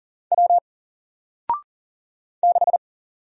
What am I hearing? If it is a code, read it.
Morse code, heard as a keyed tone, about 40 wpm, 700 Hz. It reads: W 6
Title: Single Letter-Number 40wpm